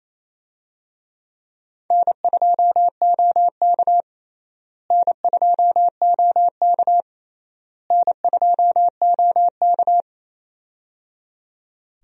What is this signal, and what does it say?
Morse code, heard as a keyed tone, about 28 wpm, 700 Hz. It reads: N2OK N2OK N2OK
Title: USA Call Signs - 4 Characters - Repeated 3x - 28wpm